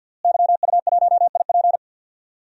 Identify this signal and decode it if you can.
Morse code, heard as a keyed tone, about 50 wpm, 700 Hz. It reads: YU1IP
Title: DXCC-100 Call Signs - 5 Characters - Repeated 3x - 50wpm